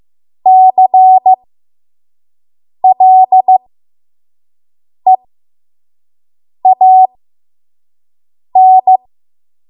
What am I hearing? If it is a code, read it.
Morse code, heard as a keyed tone, about 15 wpm, spaced as for 5 wpm, 750 Hz. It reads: CLEAN
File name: Morse / 05WPM